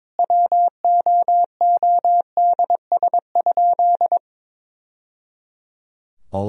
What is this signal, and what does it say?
Morse code, heard as a keyed tone, about 21 wpm, 700 Hz. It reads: WOODS?
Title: Sentences from Top 1000 Words - Review Part 2 - 22wpm